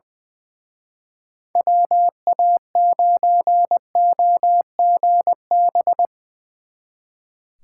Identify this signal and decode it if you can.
Morse code, heard as a keyed tone, about 22 wpm, 700 Hz. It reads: WA9OGB
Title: Speed Racing - USA 2x3 Call Signs 20wpm